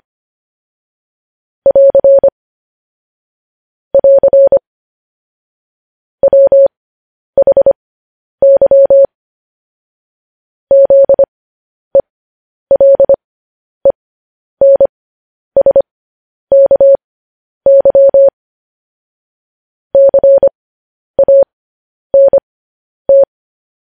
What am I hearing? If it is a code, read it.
Morse code, heard as a keyed tone, about 25 wpm, spaced as for 10 wpm, 550 Hz. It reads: <AR> <AR> WHY ZELENSKY CANT
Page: news headlines in morse code at 10 wpm